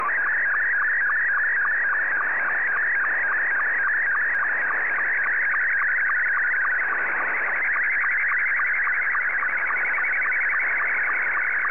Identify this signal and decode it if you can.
SSTV (slow-scan television), Scottie 2.
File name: sstv / S2-VK7FGGT